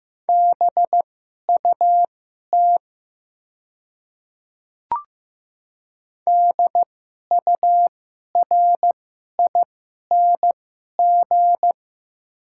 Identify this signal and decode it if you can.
Morse code, heard as a keyed tone, about 15 wpm, spaced as for 11 wpm, 700 Hz. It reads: BUT DURING
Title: 2x Character Spacing - Top 500 Words - 15wpm